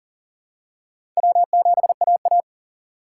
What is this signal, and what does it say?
Morse code, heard as a keyed tone, about 37 wpm, 700 Hz. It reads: W7AA
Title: USA Call Signs - 4 Characters - 4x Character Spacing - 40wpm